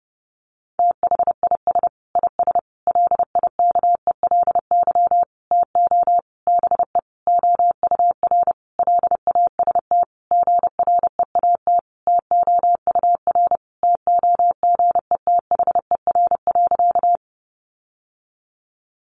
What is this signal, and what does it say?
Morse code, heard as a keyed tone, about 30 wpm, 700 Hz. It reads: THIS IS LIKELY TO BE OUR LAST GREAT TOUR TOGETHER.
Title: Sentences from Top 1400 Words 30wpm